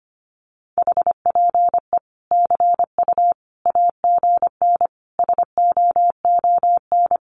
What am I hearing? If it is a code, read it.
Morse code, heard as a keyed tone, about 25 wpm, 700 Hz. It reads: HPE CU AGN SOON
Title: Common QSO Phrases 25wpm